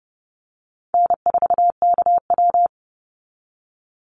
Morse code, heard as a keyed tone, about 30 wpm, 700 Hz. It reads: N4KW
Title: Call Signs Repeated 3x - 30wpm